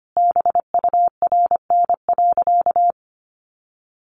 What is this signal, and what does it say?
Morse code, heard as a keyed tone, about 25 wpm, 700 Hz. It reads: BURN.
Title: Sentences from Top 1300 Words 25wpm